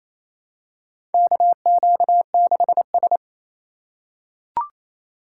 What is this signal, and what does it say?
Morse code, heard as a keyed tone, about 28 wpm, 700 Hz. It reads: KQ6S
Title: USA Call Signs - 4 Characters - 3x Character Spacing - 28wpm